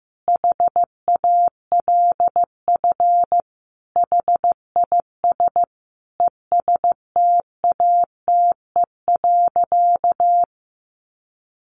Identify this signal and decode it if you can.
Morse code, heard as a keyed tone, about 15 wpm, 700 Hz. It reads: HALF HIS ESTATE.
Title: Sentences from Top 1200 Words 15wpm